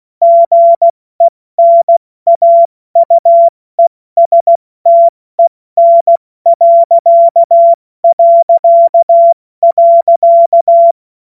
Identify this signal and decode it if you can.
Morse code, heard as a keyed tone, about 16 wpm, 680 Hz. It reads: GENAUESTEN...
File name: Morsekurs / Witze / Witz 16. Dezember 2025 70 F80